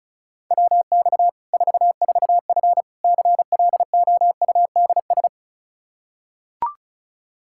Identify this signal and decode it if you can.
Morse code, heard as a keyed tone, about 36 wpm, 700 Hz. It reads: WX 44F CLOUDS
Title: K1USN SST QSO Practice 35wpm